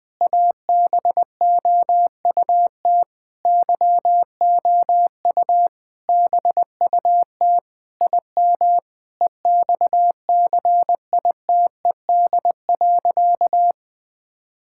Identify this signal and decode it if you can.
Morse code, heard as a keyed tone, about 20 wpm, 700 Hz. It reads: ABOUT YOU BUT IM EXCITED.
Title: Sentences with 6-Word Phrases - 20wpm